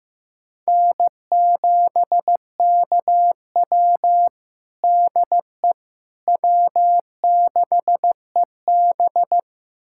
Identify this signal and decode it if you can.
Morse code, heard as a keyed tone, about 15 wpm, 700 Hz. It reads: N7KW DE W6EB
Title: USA Callsign DE Callsign - 4 Characters - 15wpm